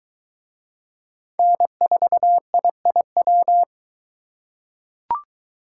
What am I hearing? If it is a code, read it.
Morse code, heard as a keyed tone, about 21 wpm, 700 Hz. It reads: N4IIW